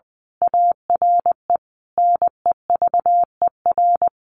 Morse code, heard as a keyed tone, about 20 wpm, 700 Hz. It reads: ARE NEVER
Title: A Sound Of Thunder 20wpm